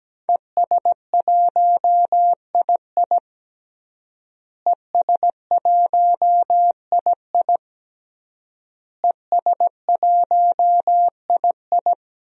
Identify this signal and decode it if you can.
Morse code, heard as a keyed tone, about 17 wpm, 700 Hz. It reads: ES1II ES1II ES1II
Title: DXCC-100 Call Signs - 5 Characters - Repeated 3x - 17wpm